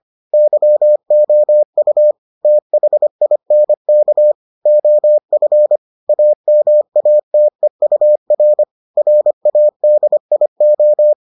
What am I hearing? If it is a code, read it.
Morse code, heard as a keyed tone, about 25 wpm, 600 Hz. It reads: YOU THINK OF AMATEUR RADIO